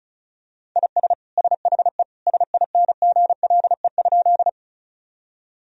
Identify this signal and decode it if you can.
Morse code, heard as a keyed tone, about 36 wpm, 700 Hz. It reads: IS SHE SINGLE?
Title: Sentences from Top 1000 Words - Review Part 4 - 35wpm